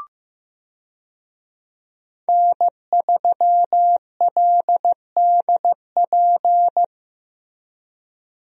Morse code, heard as a keyed tone, about 15 wpm, 700 Hz. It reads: N3LDP
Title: USA Call Signs - 5 Characters - Repeated 3x - 15wpm